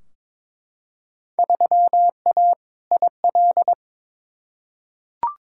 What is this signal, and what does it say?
Morse code, heard as a keyed tone, about 22 wpm, 700 Hz. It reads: 3A IL
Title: Field Day Exchanges - 22wpm